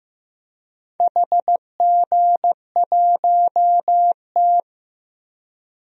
Morse code, heard as a keyed tone, about 15 wpm, 700 Hz. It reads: HG1T
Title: DXCC-100 Call Signs - 4 Characters - Repeated 3x - 15wpm